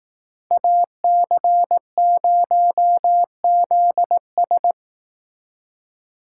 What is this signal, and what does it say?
Morse code, heard as a keyed tone, about 18 wpm, 700 Hz. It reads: AC0ZS
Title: Speed Racing - USA 2x2 Call Signs 15wpm